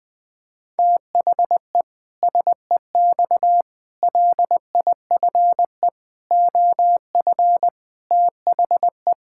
Morse code, heard as a keyed tone, about 21 wpm, 700 Hz. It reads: THE SEX LIFE OF THE
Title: Pangram Challenge v2 - 20wpm